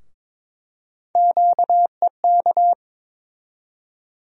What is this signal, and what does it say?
Morse code, heard as a keyed tone, about 21 wpm, 700 Hz. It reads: QEK
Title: USA Call Sign Postfixes - 3 Letters 22wpm